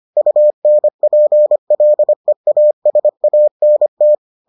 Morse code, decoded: UNPLEASANT

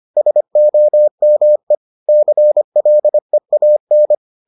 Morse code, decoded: SOME CLEAN

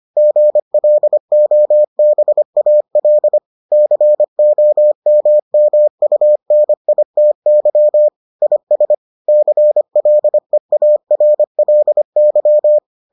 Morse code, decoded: GLOBAL COMMUNITY IS CLEARLY

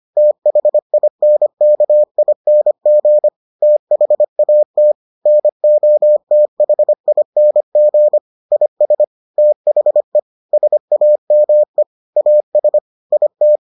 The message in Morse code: THINKING THAT NOTHING IS THE SAME AS IT

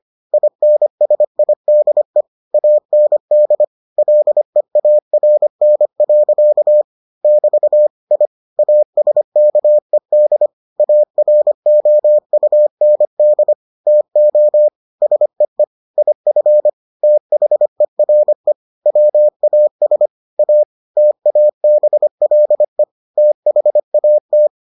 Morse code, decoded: INSIDE AND LEARN. <BT> I ASKED AROUND TO SEE IF THERE WAS A TABLE THAT

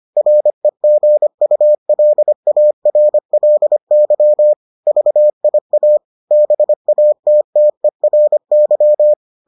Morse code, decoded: REGULARLY VIA BATTERY

